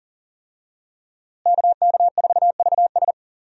Morse code, heard as a keyed tone, about 40 wpm, 700 Hz. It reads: KK4VS
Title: USA Call Signs - 5 Characters - Repeated 3x - 40wpm